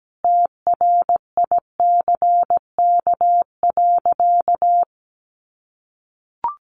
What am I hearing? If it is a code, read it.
Morse code, heard as a keyed tone, about 17 wpm, 700 Hz. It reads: TRICK.
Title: Sentences from Top 1500 Words 17wpm